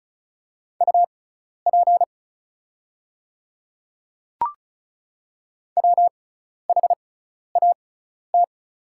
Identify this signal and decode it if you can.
Morse code, heard as a keyed tone, about 36 wpm, spaced as for 12 wpm, 700 Hz. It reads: UP WHAT
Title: 6x Character Spacing - Top 100 Words - 35wpm